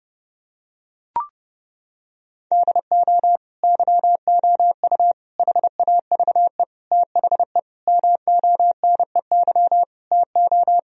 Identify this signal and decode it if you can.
Morse code, heard as a keyed tone, about 30 wpm, 700 Hz. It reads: DO YOU HAVE THE MONEY TO
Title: Sentences from Top 1000 Words - Review Part 2 - 30wpm